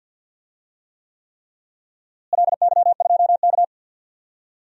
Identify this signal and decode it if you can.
Morse code, heard as a keyed tone, about 52 wpm, 700 Hz. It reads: LY2K
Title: DXCC-100 Call Signs - 4 Characters - Repeated 3x - 50wpm